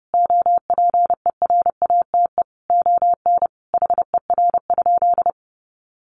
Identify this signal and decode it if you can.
Morse code, heard as a keyed tone, about 30 wpm, 700 Hz. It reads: OPERATE ON HER?
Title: Sentences from Top 1500 Words 30wpm